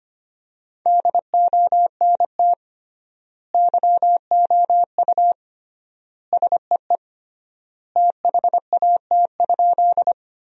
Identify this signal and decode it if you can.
Morse code, heard as a keyed tone, about 25 wpm, 700 Hz. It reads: DONT YOU SEE THAT?
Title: 3x Word-Spacing - Sentences from Top 300 Words - 25wpm